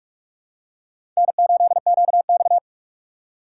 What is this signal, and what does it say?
Morse code, heard as a keyed tone, about 43 wpm, 700 Hz. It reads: N8QX
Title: USA Call Signs - 4 Characters - 2x Character Spacing - 45wpm